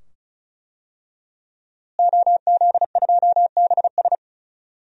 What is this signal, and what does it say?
Morse code, heard as a keyed tone, about 36 wpm, 700 Hz. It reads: OZ2BS
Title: DXCC-100 Call Signs - 5 Characters - Repeated 2x - 35wpm